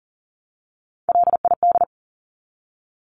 Morse code, heard as a keyed tone, about 40 wpm, 700 Hz. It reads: LID
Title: Abbreviations 40wpm